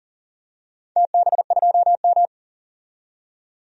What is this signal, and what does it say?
Morse code, heard as a keyed tone, about 40 wpm, 700 Hz. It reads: TB2M
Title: DXCC-100 Call Signs - 4 Characters - 40wpm